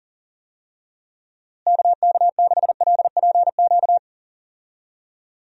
Morse code, heard as a keyed tone, about 40 wpm, 700 Hz. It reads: KK6LPQ
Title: USA Call Signs - 6 Characters - Repeated 2x - 40wpm